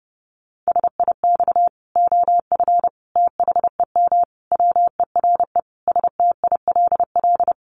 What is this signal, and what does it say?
Morse code, heard as a keyed tone, about 30 wpm, 700 Hz. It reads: SIX OF THEM WERE STILL
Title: Sentences from Top 1100 Words 30wpm